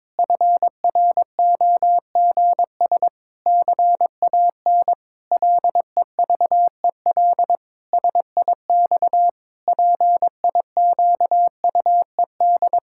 Morse code, heard as a keyed tone, about 22 wpm, 700 Hz. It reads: FROGS CAN LEVEL SIX PIQUED
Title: Pangram Challenge v2 - 22wpm